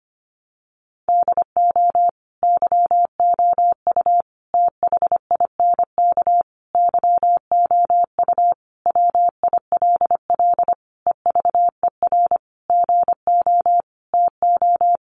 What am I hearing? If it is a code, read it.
Morse code, heard as a keyed tone, about 25 wpm, 700 Hz. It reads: DO YOU THINK YOU WILL EVER GO TO
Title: Sentences from Top 1300 Words 25wpm